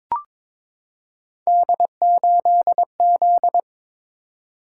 Morse code, heard as a keyed tone, about 22 wpm, 700 Hz. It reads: D8Z
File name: practice / 22wpm / Three Letter-Number Combinations - 22wpm